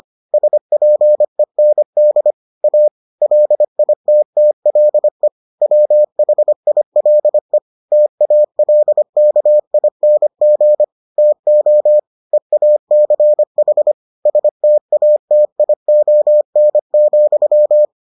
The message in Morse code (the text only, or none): SPEND A LITTLE WHILE TALKING TO EACH STATION,